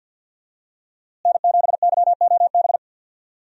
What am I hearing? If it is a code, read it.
Morse code, heard as a keyed tone, about 47 wpm, 700 Hz. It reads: N7YOB
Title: USA Call Signs - 5 Characters - Repeated 2x - 50wpm